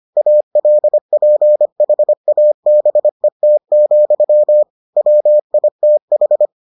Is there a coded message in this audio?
Yes: Morse code from 0.2 to 6.5 s, heard as a keyed tone, about 25 wpm, 600 Hz. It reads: ALPHABET, WITH